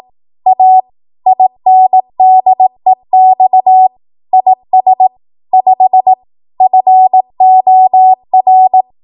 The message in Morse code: A INDEX IS 5 FOR